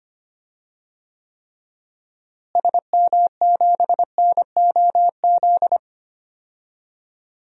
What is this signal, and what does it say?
Morse code, heard as a keyed tone, about 25 wpm, 700 Hz. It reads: SM7NOZ
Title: DXCC-100 Call Signs - 6 Characters - Repeated 3x - 25wpm